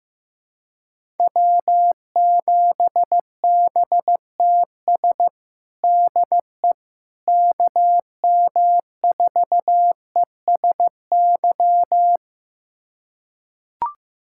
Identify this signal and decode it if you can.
Morse code, heard as a keyed tone, about 15 wpm, 700 Hz. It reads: W7BTS DE KM4ESY
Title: USA Callsign DE Callsign 15wpm